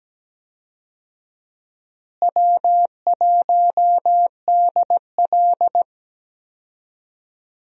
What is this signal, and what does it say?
Morse code, heard as a keyed tone, about 17 wpm, 700 Hz. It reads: W1DL
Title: USA Call Signs - 4 Characters - Repeated 2x - 17wpm